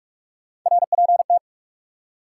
Morse code, heard as a keyed tone, about 45 wpm, 700 Hz. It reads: RPT
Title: Top 180 QSO Elements 45wpm